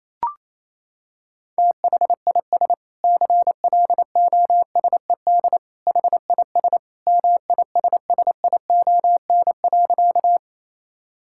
Morse code, heard as a keyed tone, about 28 wpm, 700 Hz. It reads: THIS CLOSED HIS MISSION.
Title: Sentences from Top 1600 Words 28wpm